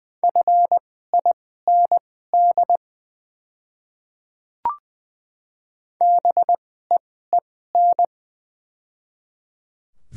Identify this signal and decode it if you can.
Morse code, heard as a keyed tone, about 21 wpm, spaced as for 15 wpm, 700 Hz. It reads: FIND BEEN
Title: 2x Character Spacing - Top 200 Words - 20wpm